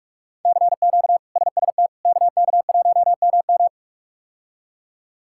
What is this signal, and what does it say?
Morse code, heard as a keyed tone, about 46 wpm, 700 Hz. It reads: CQ SST KK1MM
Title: K1USN SST QSO Practice - 45wpm